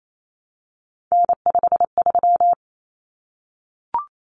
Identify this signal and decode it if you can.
Morse code, heard as a keyed tone, about 28 wpm, 700 Hz. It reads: N53
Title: Common Cut Numbers - 3-Digit 28wpm